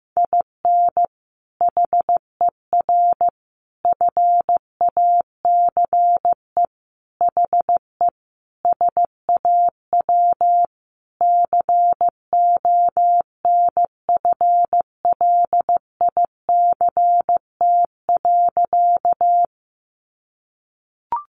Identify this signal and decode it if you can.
Morse code, heard as a keyed tone, about 15 wpm, 700 Hz. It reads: IN HER FACE HE SAW CONFLICT.
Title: Sentences from Top 1300 Words 15wpm